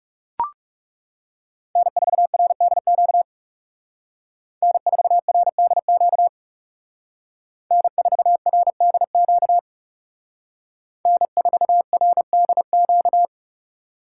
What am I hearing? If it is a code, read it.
Morse code, heard as a keyed tone, about 38 wpm, 700 Hz. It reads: N4RDQ N4RDQ N4RDQ N4RDQ